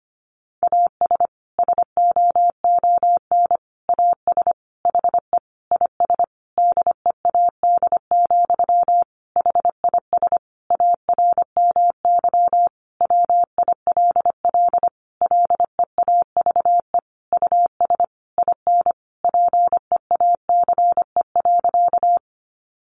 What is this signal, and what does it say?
Morse code, heard as a keyed tone, about 25 wpm, 700 Hz. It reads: AS SOON AS HE IS DEAD, HIS ARMY WILL LEAVE US IN PEACE.
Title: Sentences from Top 1500 Words 25wpm